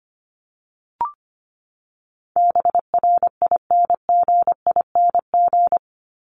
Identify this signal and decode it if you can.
Morse code, heard as a keyed tone, about 25 wpm, 700 Hz. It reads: BRINGING
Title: ING Suffix - Top 100 Words with Base Words 25wpm